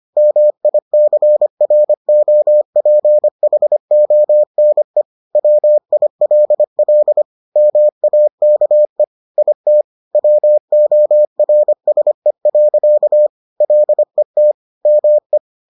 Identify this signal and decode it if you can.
Morse code, heard as a keyed tone, about 25 wpm, 600 Hz. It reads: MICROPHONE WILL MAKE IT WORSE. LET ME